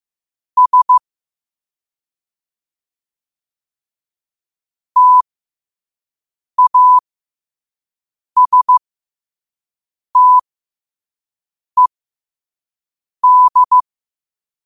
Morse code, heard as a keyed tone, about 15 wpm, spaced as for 5 wpm, 1000 Hz. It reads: S TASTED
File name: qotdfiles / 5wpm